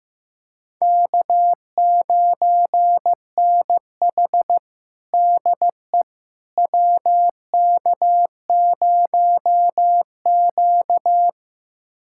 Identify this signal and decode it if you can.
Morse code, heard as a keyed tone, about 16 wpm, 700 Hz. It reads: K9NH DE WK0Q
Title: USA Callsign DE Callsign - 4 Characters - 15wpm